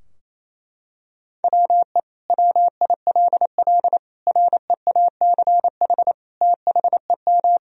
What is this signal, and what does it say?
Morse code, heard as a keyed tone, about 28 wpm, 700 Hz. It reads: WE WILL REACH THEM
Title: Sentences from Top 1000 Words - Review Part 2 - 28wpm